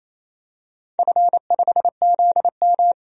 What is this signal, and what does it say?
Morse code, heard as a keyed tone, about 27 wpm, 700 Hz. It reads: F5ZM
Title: DXCC-100 Call Signs - 4 Characters - 28wpm